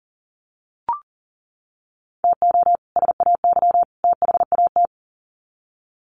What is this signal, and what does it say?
Morse code, heard as a keyed tone, about 40 wpm, 700 Hz. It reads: TO SAY THAT
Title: Sets of 3 Words 40wpm